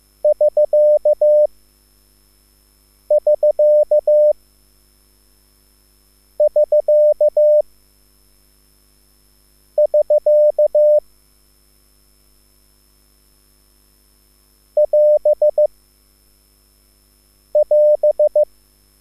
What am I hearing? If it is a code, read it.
Morse code, heard as a keyed tone, about 15 wpm, spaced as for 4 wpm, 600 Hz. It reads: <SK><SK><SK><SK> <AS><AS>
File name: QSL / CW lesson